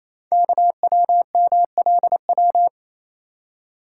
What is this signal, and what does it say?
Morse code, heard as a keyed tone, about 27 wpm, 700 Hz. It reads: KWMLW
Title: Code Groups - 5 Letters - 28wpm